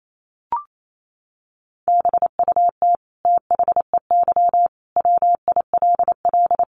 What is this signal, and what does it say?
Morse code, heard as a keyed tone, about 28 wpm, 700 Hz. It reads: BUT THEY WILL
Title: Sentences from Top 900 Words 28wpm